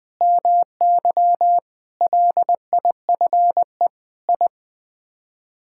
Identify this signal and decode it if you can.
Morse code, heard as a keyed tone, about 21 wpm, 700 Hz. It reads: MY LIFE I
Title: Sets of 8 Words - 20wpm